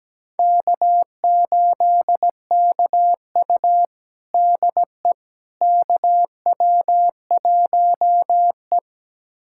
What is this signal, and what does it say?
Morse code, heard as a keyed tone, about 17 wpm, 700 Hz. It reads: K8KU DE KW1E
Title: USA Callsign DE Callsign - 4 Characters - 17wpm